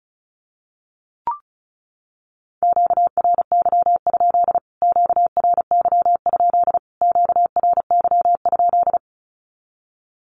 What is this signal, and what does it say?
Morse code, heard as a keyed tone, about 34 wpm, 700 Hz. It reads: QRY? QRY? QRY?
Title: Q-Codes Repeated 3x 35wpm